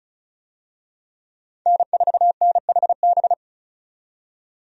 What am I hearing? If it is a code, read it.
Morse code, heard as a keyed tone, about 36 wpm, 700 Hz. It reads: N4NHB
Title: USA Call Signs - 5 Characters - 35wpm